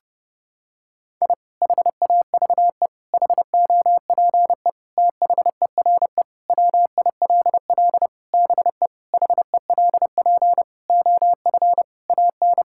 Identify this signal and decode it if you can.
Morse code, heard as a keyed tone, about 30 wpm, 700 Hz. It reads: I HAVE HOPE THERE WILL BE HELP OF AN
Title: Sentences from Top 1000 Words - Review Part 5 - 30wpm